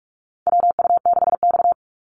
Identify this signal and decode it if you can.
Morse code, heard as a keyed tone, about 44 wpm, 700 Hz. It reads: WU6X
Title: Call Signs - 45wpm